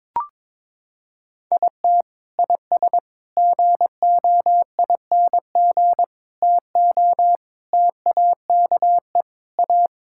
Morse code, heard as a keyed tone, about 22 wpm, 700 Hz. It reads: IT IS GOING TO TAKE A
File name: practice / 22wpm / Sentences with 4-Word Phrases - 22wpm